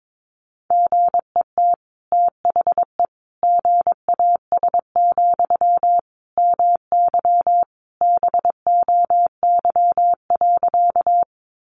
Morse code, heard as a keyed tone, about 22 wpm, 700 Hz. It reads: GET THE GAS, MY BOY.
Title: Sentences from Top 1100 Words 22wpm